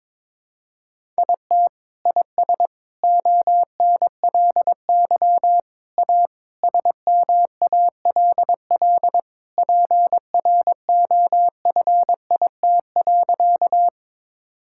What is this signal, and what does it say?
Morse code, heard as a keyed tone, about 22 wpm, 700 Hz. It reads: IT IS ONLY A SMALL PROFIT.
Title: Sentences from Top 1000 Words - Review Part 5 - 22wpm